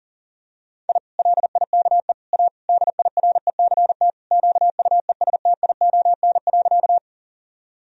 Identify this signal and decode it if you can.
Morse code, heard as a keyed tone, about 40 wpm, 700 Hz. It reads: I LIKE A DIRECT QUESTION.
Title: Sentences from Top 1000 Words - Review Part 2 - 40wpm